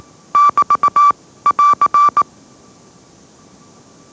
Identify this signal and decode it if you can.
Morse code, heard as a keyed tone, about 18 wpm, 1160 Hz. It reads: <BT> <AR>
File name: mrx / mrx081 20wpm